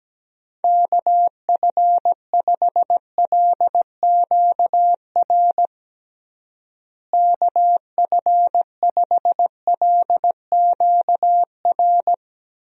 Morse code, heard as a keyed tone, about 17 wpm, 700 Hz. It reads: KF5LQR KF5LQR
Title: USA Call Signs - 6 Characters - Repeated 3x - 17wpm